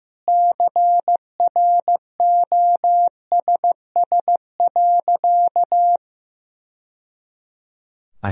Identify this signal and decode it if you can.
Morse code, heard as a keyed tone, about 16 wpm, 700 Hz. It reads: CROSS.
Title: Sentences from Top 1000 Words - Review Part 1 - 15wpm